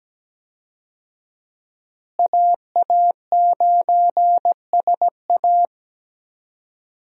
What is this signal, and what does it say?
Morse code, heard as a keyed tone, about 17 wpm, 700 Hz. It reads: AA9SA
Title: USA Call Signs - 5 Characters - Repeated 2x - 17wpm